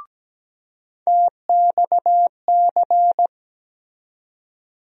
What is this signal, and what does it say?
Morse code, heard as a keyed tone, about 17 wpm, 700 Hz. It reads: TXC
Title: USA Call Sign Postfixes - 3 Letters 17wpm